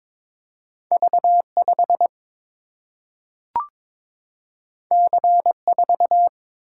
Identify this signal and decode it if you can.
Morse code, heard as a keyed tone, about 22 wpm, 700 Hz. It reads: V 5 C 4
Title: Single-Letter Single-Number 22wpm